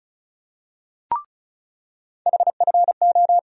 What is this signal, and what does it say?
Morse code, heard as a keyed tone, about 34 wpm, 700 Hz. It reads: HFO
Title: USA Call Sign Postfixes - 3 Letters 35wpm